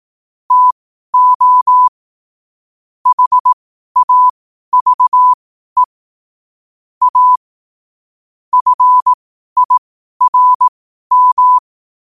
Morse code, heard as a keyed tone, about 18 wpm, spaced as for 12 wpm, 1000 Hz. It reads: TO HAVE A FIRM
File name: qotdfiles / 12wpm